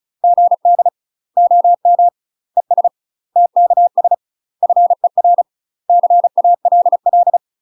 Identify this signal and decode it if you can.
Morse code, heard as a keyed tone, about 34 wpm, 700 Hz. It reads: GD OM ES TKS FER CALL
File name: UK MASTER QSOs TRAINING / 35WPM / UK MASTER QSOs 0042